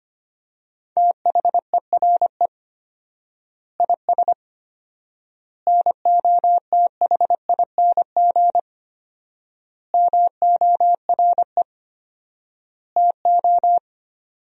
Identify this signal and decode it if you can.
Morse code, heard as a keyed tone, about 25 wpm, 700 Hz. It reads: THERE IS NOTHING MORE TO
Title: 4x Word-Spacing - Sentences from Top 500 Words - 25wpm